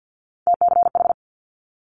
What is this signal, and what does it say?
Morse code, heard as a keyed tone, about 50 wpm, 700 Hz. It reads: TCH
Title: Multi-letter Phonemes Quick Fire 50wpm